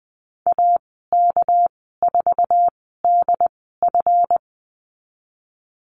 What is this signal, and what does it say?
Morse code, heard as a keyed tone, about 20 wpm, spaced as for 14 wpm, 700 Hz. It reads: AK4DF
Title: 245 Lesson 39g - Letter z - Call Signs 20wpm